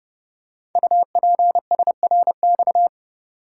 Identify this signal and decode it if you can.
Morse code, heard as a keyed tone, about 30 wpm, 700 Hz. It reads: UPSRX
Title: Code Groups - 5 Letters - 30wpm